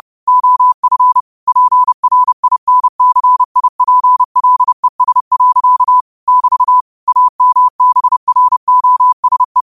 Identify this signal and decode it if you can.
Morse code, heard as a keyed tone, about 30 wpm, 1000 Hz. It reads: OF PRINCIPLES. <BT> AMBROSE
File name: qotdfiles / 30wpm